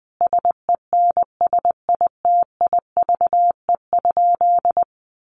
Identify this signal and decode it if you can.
Morse code, heard as a keyed tone, about 20 wpm, 700 Hz. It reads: SENSITIVE?
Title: Sentences from Top 1400 Words 20wpm